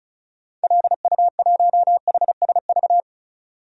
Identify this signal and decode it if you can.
Morse code, heard as a keyed tone, about 36 wpm, 700 Hz. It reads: LU1HSV
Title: DXCC-100 Call Signs - 6 Characters - 35wpm